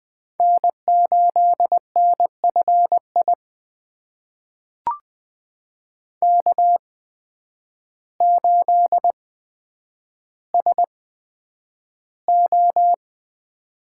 Morse code, heard as a keyed tone, about 20 wpm, spaced as for 9 wpm, 700 Hz. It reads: N8NFI K8SO
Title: USA Call Signs - 5 Characters - 8x Character Spacing - 20wpm